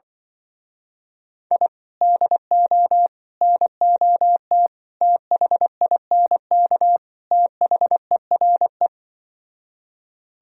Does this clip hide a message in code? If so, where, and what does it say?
Yes: Morse code from 1.5 to 8.9 s, heard as a keyed tone, about 25 wpm, 700 Hz. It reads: I DO NOT THINK THERE